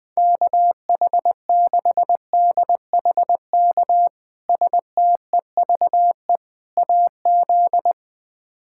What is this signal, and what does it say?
Morse code, heard as a keyed tone, about 20 wpm, 700 Hz. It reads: KH6DHK STEVE AZ
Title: K1USN SST QSO Practice - 20wpm